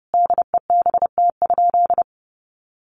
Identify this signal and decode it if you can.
Morse code, heard as a keyed tone, about 30 wpm, 700 Hz. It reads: DEBT?
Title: Sentences from Top 1300 Words 30wpm